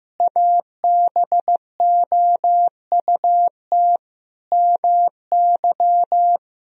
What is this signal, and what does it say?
Morse code, heard as a keyed tone, about 15 wpm, 700 Hz. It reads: ABOUT MY
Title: Sentences from Top 1000 Words - Review Part 3 - 15wpm